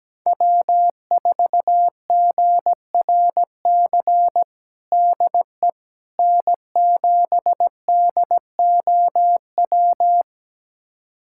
Morse code, heard as a keyed tone, about 17 wpm, 700 Hz. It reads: W4GRC DE N7DOW
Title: USA Callsign DE Callsign - 5 Characters - 17wpm